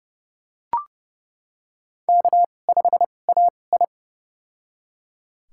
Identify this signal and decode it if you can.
Morse code, heard as a keyed tone, about 32 wpm, spaced as for 22 wpm, 700 Hz. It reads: K5AI
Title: USA Call Signs - 4 Characters - 2x Character Spacing - 30wpm